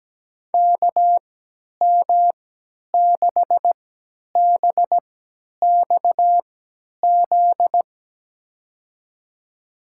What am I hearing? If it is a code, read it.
Morse code, heard as a keyed tone, about 17 wpm, spaced as for 10 wpm, 700 Hz. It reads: KM6BXZ
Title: USA Call Signs - 6 Characters - 3x Character Spacing - 17wpm